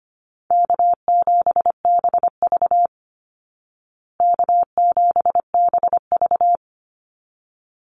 Morse code, heard as a keyed tone, about 25 wpm, 700 Hz. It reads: K7BV K7BV
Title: Call Signs Repeated 3x - 25wpm